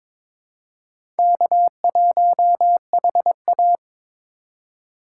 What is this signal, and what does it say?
Morse code, heard as a keyed tone, about 22 wpm, 700 Hz. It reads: K1HA
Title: USA Call Signs - 4 Characters - 3x Character Spacing - 22wpm